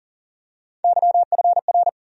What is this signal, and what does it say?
Morse code, heard as a keyed tone, about 40 wpm, 700 Hz. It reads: YFR
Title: USA Call Sign Postfixes - 3 Letters 40wpm